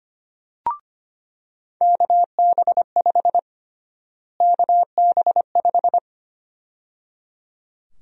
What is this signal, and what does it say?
Morse code, heard as a keyed tone, about 25 wpm, 700 Hz. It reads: KB5 KB5
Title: USA Call Sign Prefixes - Repeated 2x - 25wpm